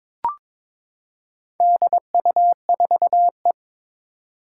Morse code, heard as a keyed tone, about 22 wpm, 700 Hz. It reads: DU4E
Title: Four Challenging Letter-Number Combinations 22wpm